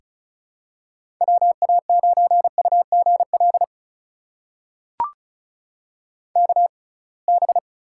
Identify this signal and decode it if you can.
Morse code, heard as a keyed tone, about 36 wpm, spaced as for 27 wpm, 700 Hz. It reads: WA9UGL KB
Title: USA Call Signs - 6x Character Spacing - 35wpm